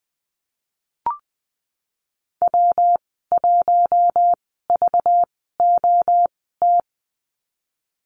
Morse code, heard as a keyed tone, about 19 wpm, spaced as for 14 wpm, 700 Hz. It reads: W1VOT
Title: 200 Lesson 32g - Letter k - Call Signs 20wpm